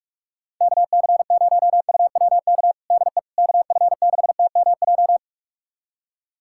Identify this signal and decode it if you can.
Morse code, heard as a keyed tone, about 44 wpm, 700 Hz. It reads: KC0UWK DE KF6TMJ
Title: USA Callsign DE Callsign - 6 Characters - 45wpm